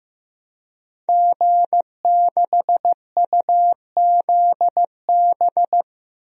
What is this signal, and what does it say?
Morse code, heard as a keyed tone, about 15 wpm, 700 Hz. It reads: G6UZB
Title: DXCC-100 Call Signs - 5 Characters - 15wpm